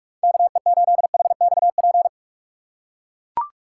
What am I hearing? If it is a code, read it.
Morse code, heard as a keyed tone, about 42 wpm, 700 Hz. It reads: KE8HXP
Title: USA Call Signs - 6 Characters - 4x Character Spacing - 45wpm